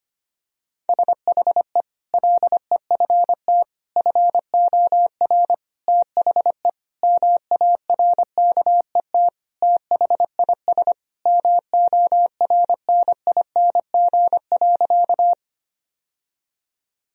Morse code, heard as a keyed tone, about 25 wpm, 700 Hz. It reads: SHE LEFT FOR THE MARKET THIS MORNING.
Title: Sentences from Top 1000 Words - Review Part 3 - 25wpm